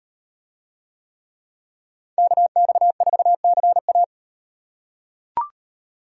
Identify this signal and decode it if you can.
Morse code, heard as a keyed tone, about 38 wpm, 700 Hz. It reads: KX4CA